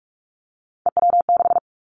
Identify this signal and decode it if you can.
Morse code, heard as a keyed tone, about 46 wpm, 700 Hz. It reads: EW6
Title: Callsign Prefixes for Top 50 DXCC - 45wpm